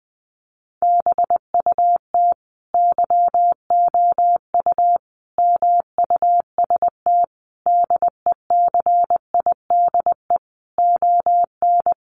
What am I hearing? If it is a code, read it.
Morse code, heard as a keyed tone, about 19 wpm, 700 Hz. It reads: BUT YOU MUST DECIDE ON
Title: Sentences from Top 600 Words 20wpm